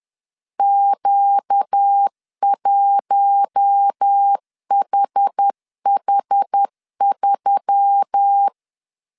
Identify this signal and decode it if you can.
Morse code, heard as a keyed tone, about 10 wpm, 800 Hz. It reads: Q1HH3
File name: CW / DET / 1 9 1500C S B 10wpm 4439